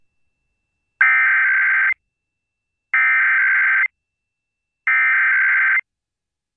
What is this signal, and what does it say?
Emergency Alert System (SAME) tones (header bursts), heard as harsh buzzing bursts. It is heard from 1.0 to 5.8 s.